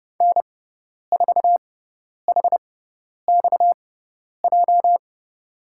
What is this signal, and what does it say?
Morse code, heard as a keyed tone, about 31 wpm, spaced as for 10 wpm, 700 Hz. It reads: N4HXJ
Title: USA Call Signs - 5 Characters - 6x Character Spacing - 30wpm